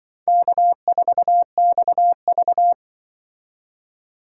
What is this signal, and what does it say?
Morse code, heard as a keyed tone, about 24 wpm, 700 Hz. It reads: K4XV
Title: Speed Racing - USA 1x2 Call Signs 20wpm